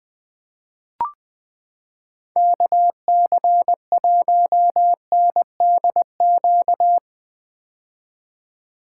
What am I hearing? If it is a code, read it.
Morse code, heard as a keyed tone, about 20 wpm, 700 Hz. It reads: KC1NDQ
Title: 2023 POTA Activator Call Signs - 20wpm